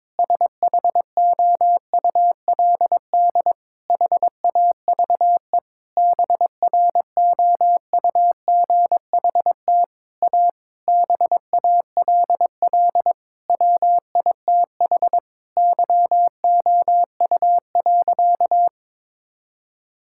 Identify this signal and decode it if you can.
Morse code, heard as a keyed tone, about 22 wpm, 700 Hz. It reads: SHOULD HAVE BROUGHT A BALL WITH YOU.
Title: Sentences from Top 1000 Words - Review Part 6 - 22wpm